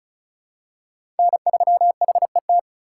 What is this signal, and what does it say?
Morse code, heard as a keyed tone, about 34 wpm, 700 Hz. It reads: N3HET
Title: USA Call Signs - 5 Characters - 4x Character Spacing - 35wpm